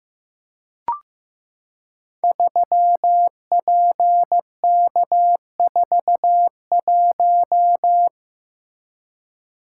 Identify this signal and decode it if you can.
Morse code, heard as a keyed tone, about 15 wpm, 700 Hz. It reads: 3PK41
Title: Code Groups - 5 Letter:Numbers - 15wpm